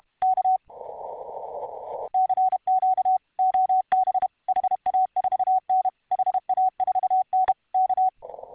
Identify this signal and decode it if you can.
Morse code, heard as a keyed tone, about 32 wpm, 750 Hz. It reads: K CQ OB HA4N HA4N K